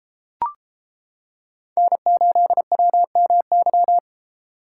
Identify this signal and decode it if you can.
Morse code, heard as a keyed tone, about 30 wpm, 700 Hz. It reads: N8WMY